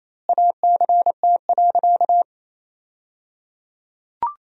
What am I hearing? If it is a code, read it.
Morse code, heard as a keyed tone, about 28 wpm, 700 Hz. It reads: ACT.
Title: 3x Word-Spacing - Sentences from Top 200 Words - 28wpm